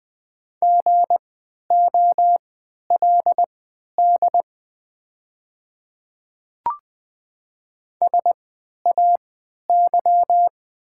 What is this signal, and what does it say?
Morse code, heard as a keyed tone, about 20 wpm, spaced as for 11 wpm, 700 Hz. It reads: GOLD SAY
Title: 3x Character Spacing - Top 500 Words - 20wpm